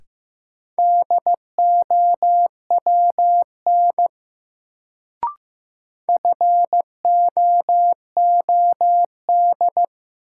Morse code, heard as a keyed tone, about 15 wpm, 700 Hz. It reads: DOWN FOOD
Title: Sentences from Top 300 Words - New Word Drill - 15wpm